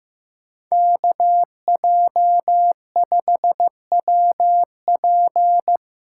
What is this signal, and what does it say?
Morse code, heard as a keyed tone, about 15 wpm, 700 Hz. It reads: KJ5WP
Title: USA Call Signs - 6x Character Spacing - 15wpm